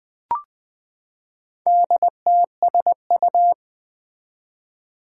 Morse code, heard as a keyed tone, about 20 wpm, 700 Hz. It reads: DTSU
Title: Four Challenging Letter-Number Combinations 20wpm